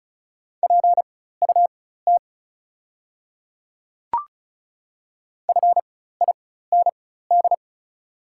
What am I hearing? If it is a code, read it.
Morse code, heard as a keyed tone, about 36 wpm, spaced as for 16 wpm, 690 Hz. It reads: PUT FIND
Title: 4x Character Spacing - Top 300 Words - 35wpm